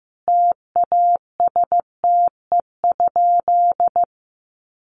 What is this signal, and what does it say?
Morse code, heard as a keyed tone, about 15 wpm, 700 Hz. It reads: TASTE?
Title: Sentences from Top 1100 Words 15wpm